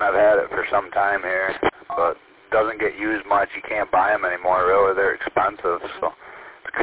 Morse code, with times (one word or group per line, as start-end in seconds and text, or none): none